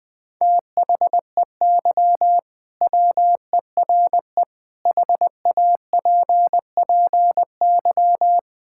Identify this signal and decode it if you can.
Morse code, heard as a keyed tone, about 21 wpm, 700 Hz. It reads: THEY WERE HAPPY
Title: Sentences with 2-Word Phrases - 20wpm